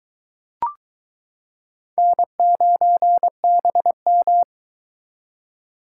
Morse code, heard as a keyed tone, about 22 wpm, 700 Hz. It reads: N9BM